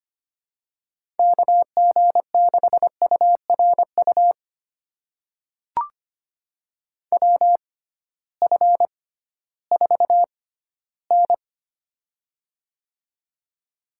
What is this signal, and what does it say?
Morse code, heard as a keyed tone, about 25 wpm, spaced as for 15 wpm, 700 Hz. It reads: KG6URU WF4N
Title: USA Call Signs - 6x Character Spacing - 25wpm